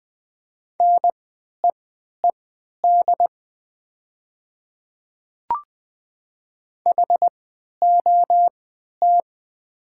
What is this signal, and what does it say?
Morse code, heard as a keyed tone, about 20 wpm, spaced as for 11 wpm, 700 Hz. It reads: NEED HOT
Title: 3x Character Spacing - Top 200 Words - 20wpm